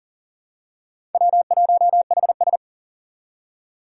Morse code, heard as a keyed tone, about 40 wpm, 700 Hz. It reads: W1HS
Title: USA Call Signs - 4 Characters - 2x Character Spacing - 40wpm